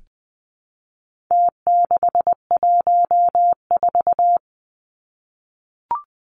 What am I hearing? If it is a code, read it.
Morse code, heard as a keyed tone, about 20 wpm, 700 Hz. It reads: T614
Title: Numbers - Contest Style - 4-Digit 20wpm